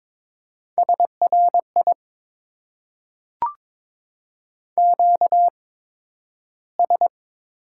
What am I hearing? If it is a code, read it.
Morse code, heard as a keyed tone, about 22 wpm, spaced as for 12 wpm, 700 Hz. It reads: SRI QS
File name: Top 90 QSO Elements - 8x Character Spacing - 22wpm